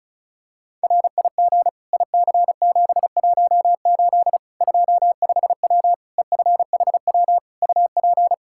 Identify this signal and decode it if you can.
Morse code, heard as a keyed tone, about 36 wpm, 700 Hz. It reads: RIG IC718 25W EFHW UP
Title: Practice QSO - Exchange 2 - Info - 35wpm